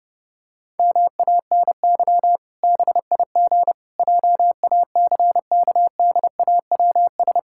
Morse code, heard as a keyed tone, about 32 wpm, 700 Hz. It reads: MANY BIG JACKDAWS
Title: Pangram Challenge v2 - 30wpm